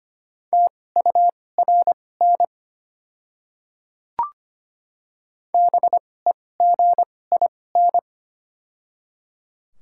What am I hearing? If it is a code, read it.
Morse code, heard as a keyed tone, about 25 wpm, spaced as for 18 wpm, 690 Hz. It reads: TURN BEGIN
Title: 2x Character Spacing - Top 300 Words - 25wpm